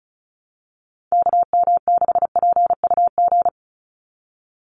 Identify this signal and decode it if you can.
Morse code, heard as a keyed tone, about 36 wpm, 700 Hz. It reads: KM6PUG
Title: USA Call Signs 35wpm